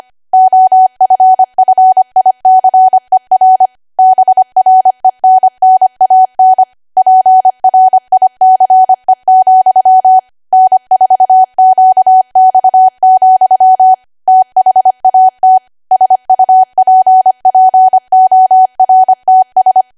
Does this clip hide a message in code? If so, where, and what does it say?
Yes: Morse code from 0.3 to 19.8 s, heard as a keyed tone, about 25 wpm, 750 Hz. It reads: OFFICER BRENNAN PRICE, N4QX, THAT SUPPORTS